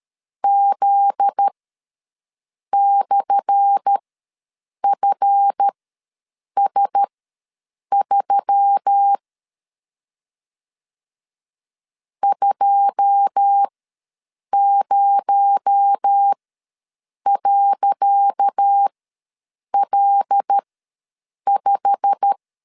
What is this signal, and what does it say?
Morse code, heard as a keyed tone, about 13 wpm, spaced as for 7 wpm, 800 Hz. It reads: Z/FS3 20.L5
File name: LEZIONI / 10 SIGNS / 10C DOT ws 12wpm 3548